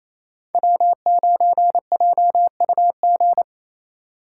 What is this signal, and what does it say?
Morse code, heard as a keyed tone, about 28 wpm, 700 Hz. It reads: W9JUG
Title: K1USN SST QSO Practice - 28wpm